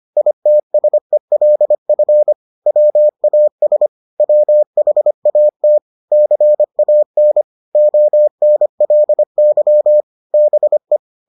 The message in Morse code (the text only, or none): ITSELF WAS WHAT CAN ONLY BE